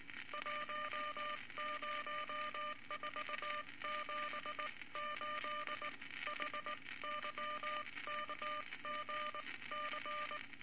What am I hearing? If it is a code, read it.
Morse code, heard as a keyed tone, about 20 wpm, spaced as for 17 wpm, 1200 Hz, with noise. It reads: 10478 HYKGC
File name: Morse Code Records / 19 B LCM 167 Twelve Words Per Minute